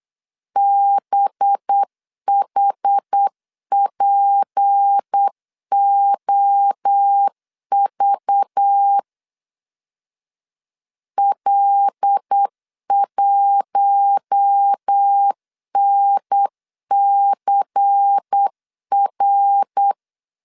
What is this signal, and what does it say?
Morse code, heard as a keyed tone, about 8 wpm, 800 Hz. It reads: BHPOV L1NCR
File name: LEZIONI / 6 PKX0 / 6 PKX0 8wpm 3341